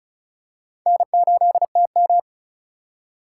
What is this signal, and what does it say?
Morse code, heard as a keyed tone, about 36 wpm, 700 Hz. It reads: N8TM
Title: USA Call Signs - 3x Character Spacing - 35wpm